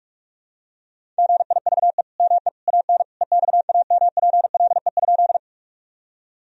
Morse code, heard as a keyed tone, about 41 wpm, 700 Hz. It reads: GIVE ME AN EXAMPLE?
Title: Sentences from Top 1000 Words - Review Part 1 - 45wpm